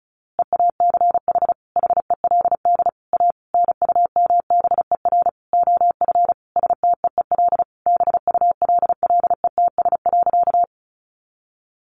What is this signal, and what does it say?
Morse code, heard as a keyed tone, about 34 wpm, 700 Hz. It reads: EACH HELD A NUMBER OF STEEL BULLETS.
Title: A Sound Of Thunder 35wpm